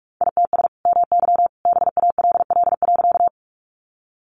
Morse code, heard as a keyed tone, about 44 wpm, 700 Hz. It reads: ITS MY BALL.
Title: Sentences from Top 900 Words 45wpm